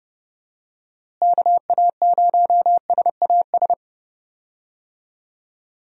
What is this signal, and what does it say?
Morse code, heard as a keyed tone, about 31 wpm, 700 Hz. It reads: KA0SAS
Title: USA Call Signs - 6 Characters - Repeated 3x - 30wpm